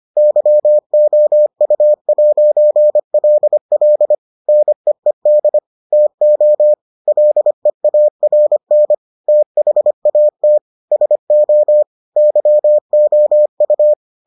Morse code, heard as a keyed tone, about 25 wpm, 600 Hz. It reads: YOU'LL NEED TO LEARN THAT SO YOU